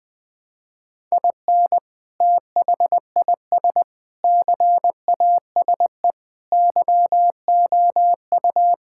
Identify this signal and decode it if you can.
Morse code, heard as a keyed tone, about 20 wpm, 700 Hz. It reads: IN THIS CASE YOU
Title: Sentences with 3-Word Phrases - Spoken Beginning - 20wpm